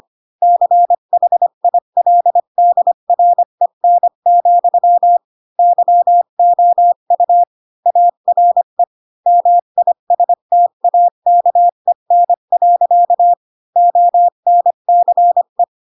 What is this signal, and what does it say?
Morse code, heard as a keyed tone, about 25 wpm, 700 Hz. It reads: CHILDREN, YOU ARE MISTAKEN. ONCE
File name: Pinocchio / Pinocchio ch1 20 25 ik1hge